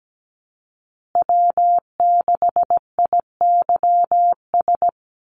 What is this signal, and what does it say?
Morse code, heard as a keyed tone, about 17 wpm, 700 Hz. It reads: W6IYS
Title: Call Signs Repeated 2x - 17wpm